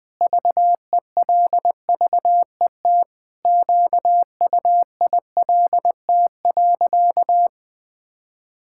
Morse code, heard as a keyed tone, about 20 wpm, 700 Hz. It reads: VELVET QUILT.
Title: Pangram Challenge v2 - 20wpm